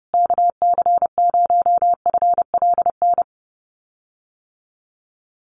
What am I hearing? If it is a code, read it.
Morse code, heard as a keyed tone, about 28 wpm, 700 Hz. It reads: KC0FLN
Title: USA Call Signs Repeated 3x 30wpm